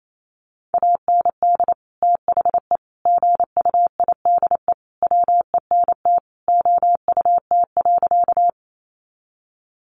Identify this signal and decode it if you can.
Morse code, heard as a keyed tone, about 28 wpm, 700 Hz. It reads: AND THE GUIDE WENT OUT.
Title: Sentences from Top 600 Words 28wpm